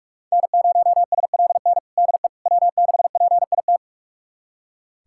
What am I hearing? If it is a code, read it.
Morse code, heard as a keyed tone, about 42 wpm, 700 Hz. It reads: N0SLN DE W6PIT
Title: USA Callsign DE Callsign - 5 Characters - 45wpm